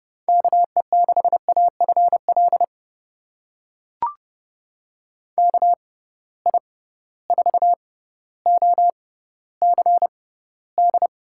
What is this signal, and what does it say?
Morse code, heard as a keyed tone, about 32 wpm, spaced as for 16 wpm, 700 Hz. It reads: KE6AFL KI4OCD
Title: USA Call Signs - 6 Characters - 6x Character Spacing - 30wpm